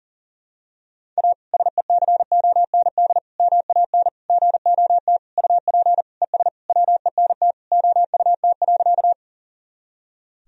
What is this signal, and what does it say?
Morse code, heard as a keyed tone, about 40 wpm, 700 Hz. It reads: A SECOND MAN GOT UP ES WENT OUT.
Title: Sentences from Top 300 Words and Abbreviations 40wpm